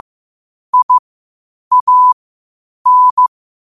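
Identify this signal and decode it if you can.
Morse code, heard as a keyed tone, about 15 wpm, spaced as for 8 wpm, 1000 Hz. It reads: IAN
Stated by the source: Morse code at 8 wpm.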